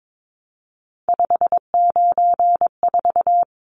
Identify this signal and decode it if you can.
Morse code, heard as a keyed tone, about 22 wpm, 700 Hz. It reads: 594
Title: Numbers - 3-Digit 22wpm